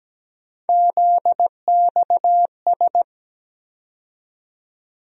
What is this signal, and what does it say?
Morse code, heard as a keyed tone, about 17 wpm, 700 Hz. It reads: ZXS
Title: USA Call Sign Postfixes - 3 Letters 17wpm